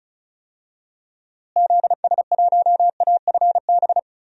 Morse code, heard as a keyed tone, about 36 wpm, 700 Hz. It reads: ZS1AFB
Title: DXCC-100 Call Signs - 6 Characters - Repeated 3x - 35wpm